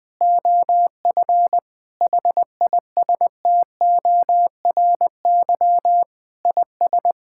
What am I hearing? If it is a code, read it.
Morse code, heard as a keyed tone, about 20 wpm, 700 Hz. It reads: OF HISTORY IS
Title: Sentences with 8-Word Phrases - Spoken Beginning - 20wpm